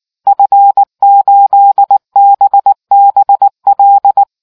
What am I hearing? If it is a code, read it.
Morse code, heard as a keyed tone, about 19 wpm, 800 Hz. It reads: F8BBL